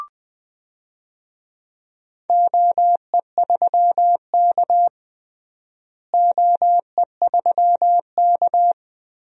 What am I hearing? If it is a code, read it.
Morse code, heard as a keyed tone, about 20 wpm, 700 Hz. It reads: OE3K OE3K
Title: DXCC-100 Call Signs - 4 Characters - Repeated 3x - 20wpm